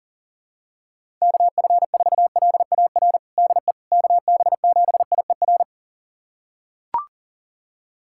Morse code, heard as a keyed tone, about 40 wpm, 700 Hz. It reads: KF4LAR DE KB7IER
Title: USA Callsign DE Callsign - 6 Characters - 40wpm